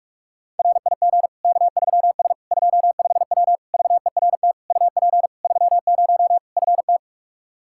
Morse code, heard as a keyed tone, about 46 wpm, 700 Hz. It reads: RIG K3S 25W VERT UP 30 FT
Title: Practice QSO - Exchange 2 - Info - 45wpm